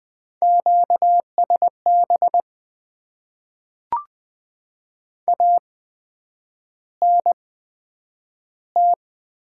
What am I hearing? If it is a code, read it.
Morse code, heard as a keyed tone, about 20 wpm, spaced as for 9 wpm, 700 Hz. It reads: QSB ANT
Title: Top 90 QSO Elements - 8x Character Spacing - 20wpm